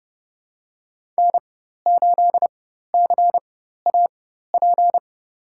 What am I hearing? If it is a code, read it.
Morse code, heard as a keyed tone, about 31 wpm, spaced as for 14 wpm, 700 Hz. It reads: N8CAP
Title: USA Call Signs - 5 Characters - 4x Character Spacing - 30wpm